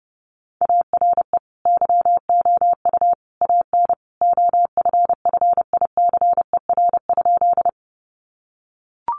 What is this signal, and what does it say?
Morse code, heard as a keyed tone, about 30 wpm, 700 Hz. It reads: ARE YOU AN OFFICER?
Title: Sentences from Top 1200 Words 30wpm